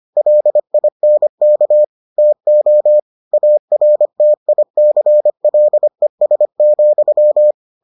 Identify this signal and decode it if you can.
Morse code, heard as a keyed tone, about 25 wpm, 600 Hz. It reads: LINK TO ARTICLES,